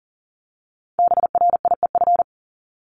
Morse code, heard as a keyed tone, about 40 wpm, 700 Hz. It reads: BRIEF
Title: LY Suffix - Top 100 Words with Base Words 40wpm